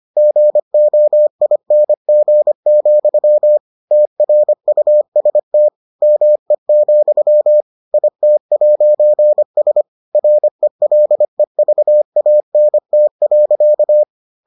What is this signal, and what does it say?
Morse code, heard as a keyed tone, about 25 wpm, 600 Hz. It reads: GOING, TRUST ME, IT'S RELEVANT.